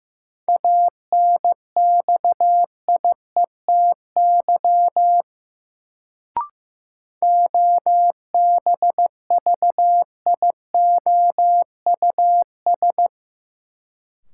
Morse code, heard as a keyed tone, about 15 wpm, 700 Hz. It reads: ANXIETY OBVIOUS
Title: Sentences from Top 1600 Words - New Word Drill - 15wpm